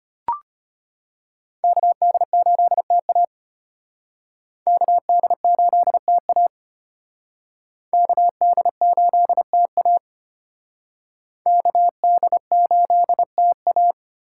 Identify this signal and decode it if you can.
Morse code, heard as a keyed tone, about 34 wpm, 700 Hz. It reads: KD8TA KD8TA KD8TA KD8TA